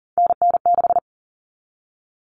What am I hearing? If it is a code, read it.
Morse code, heard as a keyed tone, about 40 wpm, 700 Hz. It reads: NN6
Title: Common Cut Numbers - 3-Digit 40wpm